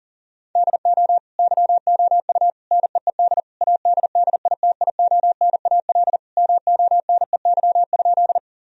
Morse code, heard as a keyed tone, about 40 wpm, 700 Hz. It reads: DO YOU NEED ADDITIONAL MONEY?
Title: Sentences from Top 1000 Words - Review Part 4 - 40wpm